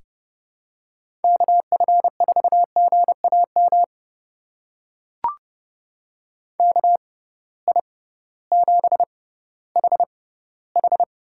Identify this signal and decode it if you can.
Morse code, heard as a keyed tone, about 30 wpm, spaced as for 16 wpm, 700 Hz. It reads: KF4GAM KI7HH
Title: USA Call Signs - 6 Characters - 6x Character Spacing - 30wpm